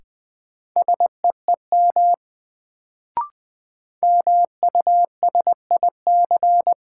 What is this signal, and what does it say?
Morse code, heard as a keyed tone, about 20 wpm, 700 Hz. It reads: SEEM MUSIC
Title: Sentences from Top 300 Words - New Word Drill - 20wpm